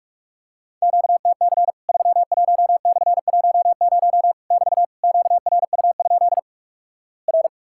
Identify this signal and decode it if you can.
Morse code, heard as a keyed tone, about 46 wpm, 700 Hz. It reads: QTC 31/10 <BT> QRU? R
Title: Worked All Europe DX Contest - QTC Exchange - 45wpm